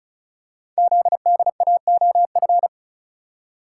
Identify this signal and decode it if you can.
Morse code, heard as a keyed tone, about 36 wpm, 700 Hz. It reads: ZDAOF
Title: Code Groups - 5 Letter:Numbers - 35wpm